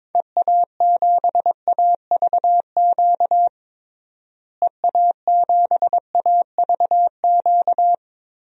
Morse code, heard as a keyed tone, about 22 wpm, 700 Hz. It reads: EA7AVQ EA7AVQ
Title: DXCC-100 Call Signs - 6 Characters - Repeated 3x - 22wpm